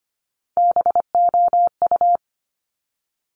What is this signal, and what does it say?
Morse code, heard as a keyed tone, about 25 wpm, 700 Hz. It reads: BOU
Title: Top Three Letter Combinations in Words 25wpm